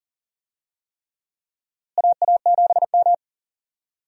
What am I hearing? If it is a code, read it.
Morse code, heard as a keyed tone, about 40 wpm, 700 Hz. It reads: AA7M
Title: USA Call Signs - 4 Characters - Repeated 3x - 40wpm